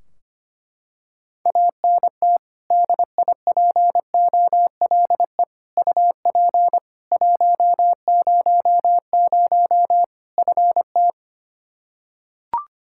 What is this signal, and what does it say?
Morse code, heard as a keyed tone, about 25 wpm, 700 Hz. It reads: ANT DIPOLE UP 100 FT
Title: Practice QSO - Exchange 2 - Info - 25wpm